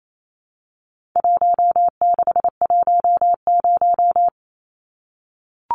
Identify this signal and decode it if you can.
Morse code, heard as a keyed tone, about 28 wpm, 700 Hz. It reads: 1610
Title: Numbers - 4-Digit 28wpm